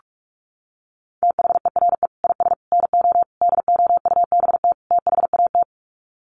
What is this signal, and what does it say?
Morse code, heard as a keyed tone, about 44 wpm, 700 Hz. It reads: THERE IS NO DOUBT THAT
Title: Sets of 5 Words 45wpm